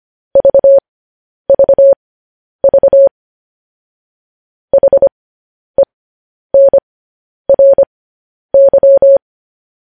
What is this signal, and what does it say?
Morse code, heard as a keyed tone, about 25 wpm, spaced as for 10 wpm, 550 Hz. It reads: VVV HENRY
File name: podcast / News headlines-1x-25x10